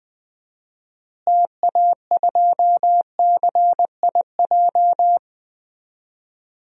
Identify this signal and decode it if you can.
Morse code, heard as a keyed tone, about 20 wpm, 700 Hz. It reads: TA2CIJ
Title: DXCC-100 Call Signs - 6 Characters - Repeated 3x - 20wpm